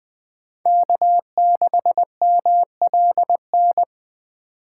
Morse code, heard as a keyed tone, about 20 wpm, 700 Hz. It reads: K6MLN
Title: USA Call Signs - 5 Characters - 8x Character Spacing - 20wpm